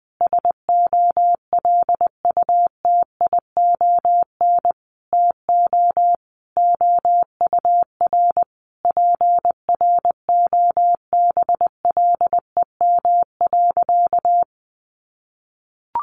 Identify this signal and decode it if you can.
Morse code, heard as a keyed tone, about 20 wpm, 700 Hz. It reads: SOLUTION TO OUR PROBLEM.
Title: Sentences from Top 1300 Words 20wpm